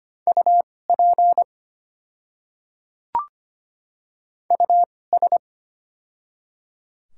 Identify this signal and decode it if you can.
Morse code, heard as a keyed tone, about 25 wpm, spaced as for 18 wpm, 700 Hz. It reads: UP US
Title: 2x Character Spacing - Top 200 Words - 25wpm